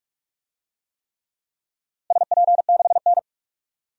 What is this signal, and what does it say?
Morse code, heard as a keyed tone, about 45 wpm, 700 Hz. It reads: SP6N
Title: DXCC-100 Call Signs - 4 Characters - Repeated 3x - 45wpm